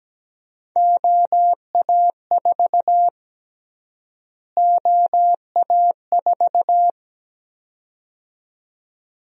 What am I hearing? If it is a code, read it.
Morse code, heard as a keyed tone, about 17 wpm, 700 Hz. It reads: OA4 OA4
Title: USA Call Sign Prefixes - Repeated 2x - 17wpm